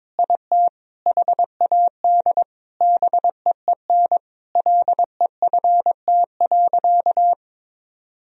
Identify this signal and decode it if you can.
Morse code, heard as a keyed tone, about 22 wpm, 700 Hz. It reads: IT HAD BEEN LEFT.
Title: Sentences from Top 1000 Words - Review Part 2 - 22wpm